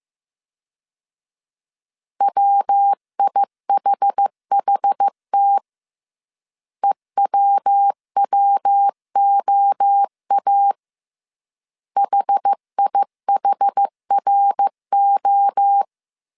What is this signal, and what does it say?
Morse code, heard as a keyed tone, about 15 wpm, 800 Hz. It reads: WIHHT EWWOA HIHRO